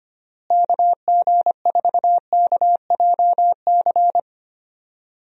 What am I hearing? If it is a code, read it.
Morse code, heard as a keyed tone, about 25 wpm, 700 Hz. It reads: KG4KJC
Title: USA Call Signs - 6 Characters - Repeated 2x - 25wpm